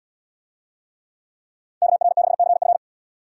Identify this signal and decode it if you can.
Morse code, heard as a keyed tone, about 68 wpm, 700 Hz. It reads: KR6CF